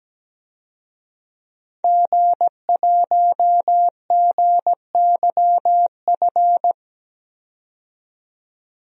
Morse code, heard as a keyed tone, about 17 wpm, 700 Hz. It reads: G1GYF
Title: DXCC-100 Call Signs - 5 Characters - 17wpm